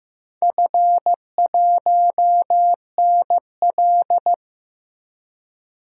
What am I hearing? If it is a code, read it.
Morse code, heard as a keyed tone, about 15 wpm, 700 Hz. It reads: F1NL
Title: DXCC-100 Call Signs - 4 Characters - 15wpm